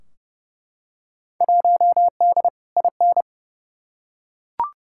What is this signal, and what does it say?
Morse code, heard as a keyed tone, about 32 wpm, 700 Hz. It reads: 1D IN
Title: Field Day Exchanges - 30wpm